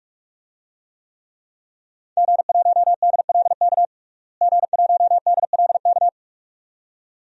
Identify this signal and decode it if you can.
Morse code, heard as a keyed tone, about 46 wpm, 700 Hz. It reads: G1DLK G1DLK
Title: DXCC-100 Call Signs - 5 Characters - Repeated 2x - 45wpm